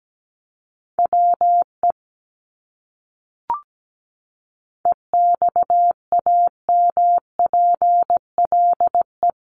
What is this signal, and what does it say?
Morse code, heard as a keyed tone, about 17 wpm, 700 Hz. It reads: WE EXAMPLE
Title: Top 300 Words Repeated 17wpm